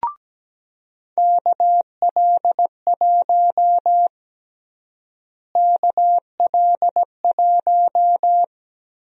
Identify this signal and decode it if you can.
Morse code, heard as a keyed tone, about 17 wpm, 700 Hz. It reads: KL1 KL1
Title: USA Call Sign Prefixes - Repeated 2x - 17wpm